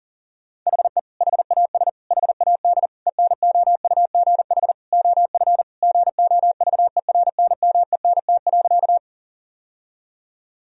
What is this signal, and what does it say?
Morse code, heard as a keyed tone, about 37 wpm, 700 Hz. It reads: HE HAS HAD ENOUGH OF GOVERNMENT.
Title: Sentences from Top 1000 Words - Review Part 3 - 40wpm